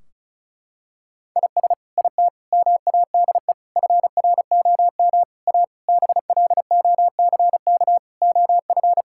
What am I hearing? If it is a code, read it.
Morse code, heard as a keyed tone, about 36 wpm, 700 Hz. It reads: IS IT MADE FROM A BLOCK OF
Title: Sentences from Top 1600 Words 35wpm